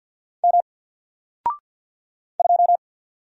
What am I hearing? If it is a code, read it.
Morse code, heard as a keyed tone, about 50 wpm, 700 Hz. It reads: M 2
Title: Single Letter-Number 50wpm